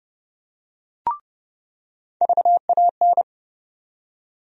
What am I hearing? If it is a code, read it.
Morse code, heard as a keyed tone, about 30 wpm, 700 Hz. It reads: VAN
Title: USA Call Sign Postfixes - 3 Letters 30wpm